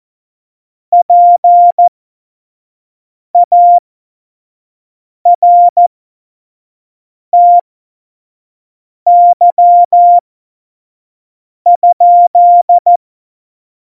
Morse code, heard as a keyed tone, about 14 wpm, spaced as for 5 wpm, 700 Hz. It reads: PARTY?
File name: Morsekurs / Witze / Witz 10.Juli 2018, 25 F70